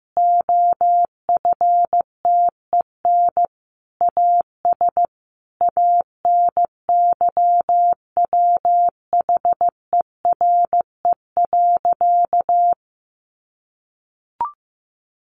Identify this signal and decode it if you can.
Morse code, heard as a keyed tone, about 15 wpm, 700 Hz. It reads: OFTEN AS ANYWHERE.
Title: Sentences from Top 1400 Words 15wpm